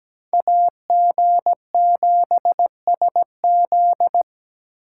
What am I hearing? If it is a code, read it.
Morse code, heard as a keyed tone, about 17 wpm, 700 Hz. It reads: AG7SZ
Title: USA Call Signs - 5 Characters - 2x Character Spacing - 17wpm